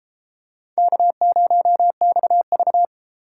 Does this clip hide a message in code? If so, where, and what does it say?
Yes: Morse code from 0.8 to 2.9 s, heard as a keyed tone, about 33 wpm, 700 Hz. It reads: K0XV